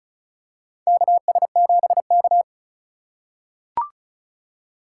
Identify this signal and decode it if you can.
Morse code, heard as a keyed tone, about 36 wpm, 700 Hz. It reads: KS7K
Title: USA Call Signs - 4 Characters - 2x Character Spacing - 35wpm